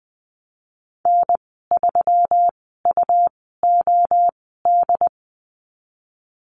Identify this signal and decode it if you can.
Morse code, heard as a keyed tone, about 20 wpm, spaced as for 15 wpm, 700 Hz. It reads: N3UOD
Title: 154 Lesson 25g - Letter p - Call Signs 20wpm